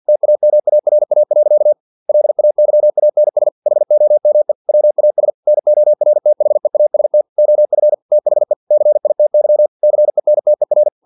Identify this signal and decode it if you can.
Morse code, heard as a keyed tone, about 50 wpm, 600 Hz. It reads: TAMARA. LAYANS HOME WAS NORTHEAST OF THE CITY CENTER
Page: Never 49wpm